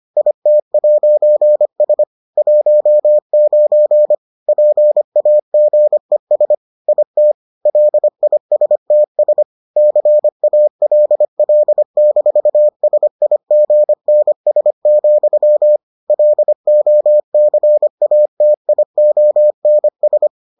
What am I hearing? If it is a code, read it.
Morse code, heard as a keyed tone, about 25 wpm, 600 Hz. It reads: IT'S 19 PAGES IT LISTS CALL-SIGNS, LOCATIONS